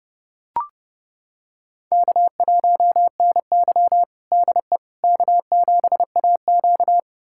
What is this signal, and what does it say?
Morse code, heard as a keyed tone, about 30 wpm, 700 Hz. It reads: K1NY DE K7AQ
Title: USA Callsign DE Callsign - 4 Characters - 30wpm